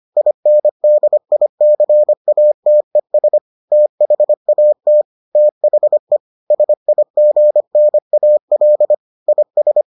Morse code, heard as a keyed tone, about 25 wpm, 600 Hz. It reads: INDICATES THAT THE SIGNAL IS